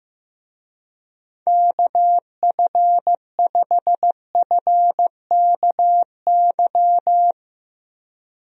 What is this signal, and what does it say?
Morse code, heard as a keyed tone, about 16 wpm, 700 Hz. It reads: KF5FKY
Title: USA Call Signs - 6 Characters - Repeated 2x - 15wpm